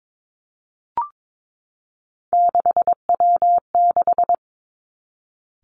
Morse code, heard as a keyed tone, about 22 wpm, 700 Hz. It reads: 6W6
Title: Three Challenging Letter-Number Combinations 22wpm